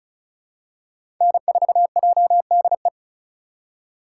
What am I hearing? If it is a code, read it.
Morse code, heard as a keyed tone, about 36 wpm, 700 Hz. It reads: N4JDE
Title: USA Call Signs - 5 Characters - 3x Character Spacing - 35wpm